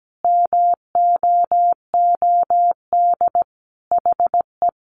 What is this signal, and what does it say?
Morse code, heard as a keyed tone, about 17 wpm, 700 Hz. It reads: MOOD HE
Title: Sentences from Top 1100 Words 17wpm